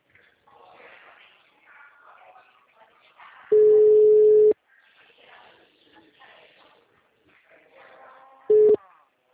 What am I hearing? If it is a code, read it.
Ringback tone (Europe, 425 Hz).